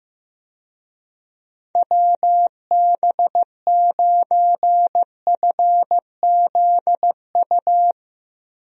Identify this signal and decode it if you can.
Morse code, heard as a keyed tone, about 16 wpm, 700 Hz. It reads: WB9FZU
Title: USA Call Signs - 6 Characters - Repeated 2x - 15wpm